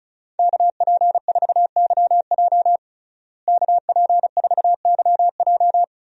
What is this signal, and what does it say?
Morse code, heard as a keyed tone, about 34 wpm, 700 Hz. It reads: KP4YJ KP4YJ
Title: DXCC-100 Call Signs - 5 Characters - Repeated 2x - 35wpm